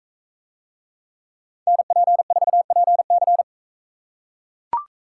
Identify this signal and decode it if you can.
Morse code, heard as a keyed tone, about 43 wpm, 700 Hz. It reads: NP4PC